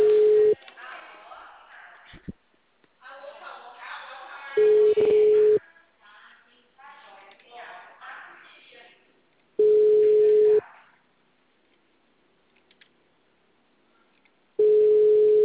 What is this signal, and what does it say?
Ringback tone (Europe, 425 Hz), with speech.